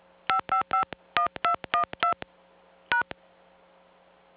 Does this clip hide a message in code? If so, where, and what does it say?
Yes: DTMF from 0.3 to 3.0 s, heard as telephone touch tones, tones of about 110 ms. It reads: 5551212 #